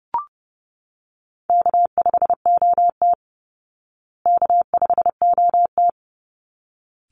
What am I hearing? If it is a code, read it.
Morse code, heard as a keyed tone, about 30 wpm, 700 Hz. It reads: K5OT K5OT
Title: Easy Call Signs Repeated 2x - 30wpm